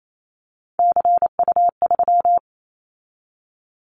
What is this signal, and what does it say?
Morse code, heard as a keyed tone, about 28 wpm, 700 Hz. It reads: CU3
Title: Callsign Prefixes for Top 100 DXCC - 28wpm